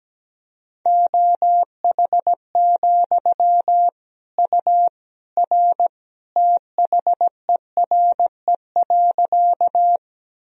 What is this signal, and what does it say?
Morse code, heard as a keyed tone, about 17 wpm, 700 Hz. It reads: OH, U R THERE.
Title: Sentences from Top 500 Words and Abbreviations 17wpm